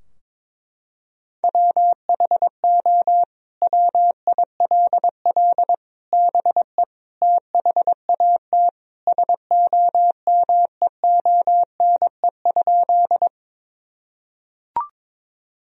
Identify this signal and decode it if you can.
Morse code, heard as a keyed tone, about 22 wpm, 700 Hz. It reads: WHO WILL BE THAT SOMEONE?
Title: Sentences from Top 1000 Words - Review Part 4 - 22wpm